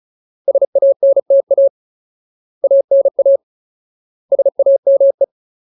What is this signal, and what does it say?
Morse code, heard as a keyed tone, about 34 wpm, 550 Hz. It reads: SANTA ANA SAME
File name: Ultimate Word List / 35WPM Ultimate